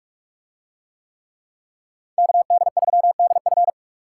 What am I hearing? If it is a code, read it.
Morse code, heard as a keyed tone, about 44 wpm, 700 Hz. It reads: KD3DF